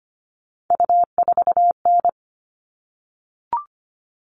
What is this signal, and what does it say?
Morse code, heard as a keyed tone, about 25 wpm, 700 Hz. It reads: U4N
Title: All Cut Numbers - 3-Digit 25wpm